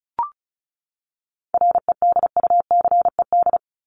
Morse code, heard as a keyed tone, about 34 wpm, 700 Hz. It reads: REDUCED
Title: ED Suffix - Top 100 Words with Base Words 35wpm